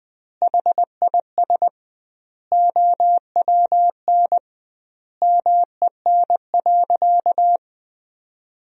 Morse code, heard as a keyed tone, about 20 wpm, 700 Hz. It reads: HIS OWN MEN.
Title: 2x Word-Spacing - Sentences from Top 300 Words - 20wpm